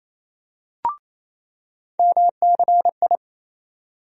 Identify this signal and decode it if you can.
Morse code, heard as a keyed tone, about 28 wpm, 700 Hz. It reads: MCI
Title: USA Call Sign Postfixes - 3 Letters 28wpm